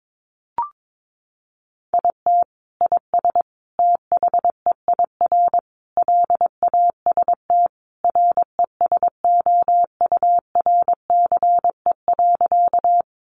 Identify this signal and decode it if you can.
Morse code, heard as a keyed tone, about 22 wpm, 700 Hz. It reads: IT IS THEIR LAST RESOURCE.
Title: Sentences from Top 1300 Words 22wpm